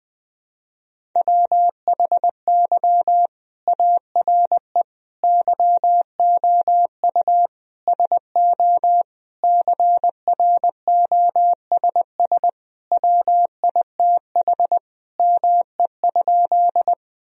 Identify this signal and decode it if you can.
Morse code, heard as a keyed tone, about 20 wpm, 700 Hz. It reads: WHY ARE YOU SO CROSS WITH ME?
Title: Sentences from Top 1000 Words - Review Part 1 - 20wpm